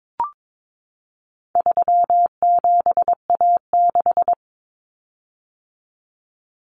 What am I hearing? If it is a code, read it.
Morse code, heard as a keyed tone, about 22 wpm, 700 Hz. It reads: 37A6
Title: Common Cut Numbers - 4-Digit 22wpm